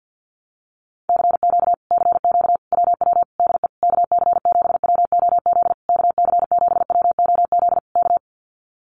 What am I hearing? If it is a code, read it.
Morse code, heard as a keyed tone, about 50 wpm, 700 Hz. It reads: CQ CQ WW DE KC7WOZ KC7WOZ K
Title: Common QSO Phrases Encore 50wpm